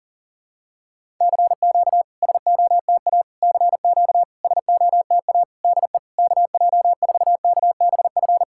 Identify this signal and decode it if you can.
Morse code, heard as a keyed tone, about 40 wpm, 700 Hz. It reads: CQ SOTA CQ SOTA DE KJ4KBF
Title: SOTA QSO Practice 40wpm